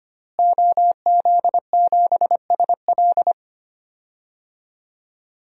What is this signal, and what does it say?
Morse code, heard as a keyed tone, about 25 wpm, 700 Hz. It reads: OZ7SL
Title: DXCC-100 Call Signs - 5 Characters - Repeated 2x - 25wpm